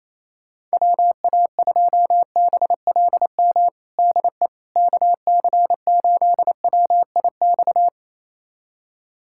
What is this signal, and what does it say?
Morse code, heard as a keyed tone, about 28 wpm, 700 Hz. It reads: WA2BLM DE KC8WIX
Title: USA Callsign DE Callsign 28wpm